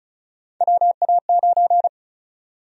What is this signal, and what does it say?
Morse code, heard as a keyed tone, about 36 wpm, 700 Hz. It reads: WA9
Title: USA Call Sign Prefixes - Repeated 2x - 35wpm